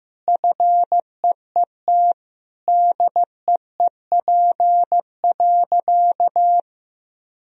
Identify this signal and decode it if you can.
Morse code, heard as a keyed tone, about 15 wpm, 700 Hz. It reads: FEET DEEP.
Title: Sentences from Top 1000 Words - Review Part 3 - 15wpm